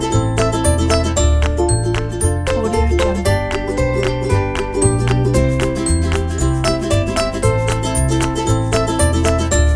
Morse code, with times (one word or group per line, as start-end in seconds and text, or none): none